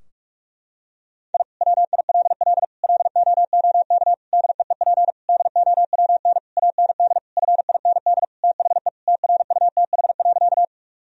Morse code, heard as a keyed tone, about 41 wpm, 700 Hz. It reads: I WILL LOOK DEEP DOWN AND FIND THE TRUTH.
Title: Sentences from Top 1000 Words - Review Part 3 - 45wpm